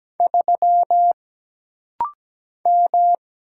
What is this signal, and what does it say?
Morse code, heard as a keyed tone, about 17 wpm, 700 Hz. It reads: 3 M
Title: Single Letter-Number Alt - Rapid-Fire 17wpm